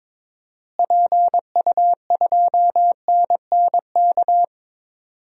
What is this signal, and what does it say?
Morse code, heard as a keyed tone, about 22 wpm, 700 Hz. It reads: PU2NNK
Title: DXCC-100 Call Signs - 6 Characters - Repeated 2x - 22wpm